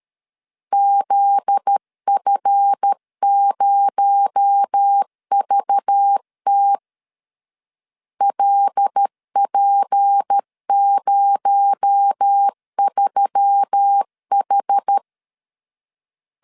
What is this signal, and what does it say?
Morse code, heard as a keyed tone, about 13 wpm, 800 Hz. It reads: ZF0VT LP03H
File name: LEZIONI / 7 GF9Z / 7 GF9Z 12wpm 2147